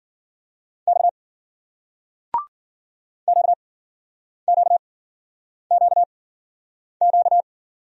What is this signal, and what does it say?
Morse code, heard as a keyed tone, about 55 wpm, spaced as for 9 wpm, 700 Hz. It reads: X QQQQ